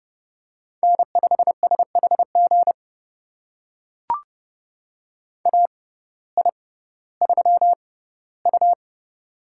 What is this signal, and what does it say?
Morse code, heard as a keyed tone, about 30 wpm, spaced as for 16 wpm, 700 Hz. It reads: N5SHG AI3U
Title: USA Call Signs - 6x Character Spacing - 30wpm